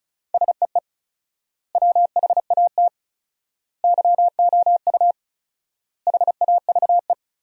Morse code, heard as a keyed tone, about 36 wpm, 700 Hz. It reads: SEE WHAT YOU HAVE
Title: 4x Word-Spacing - Sentences from Top 500 Words - 35wpm